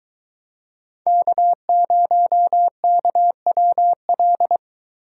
Morse code, heard as a keyed tone, about 22 wpm, 700 Hz. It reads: K0KWL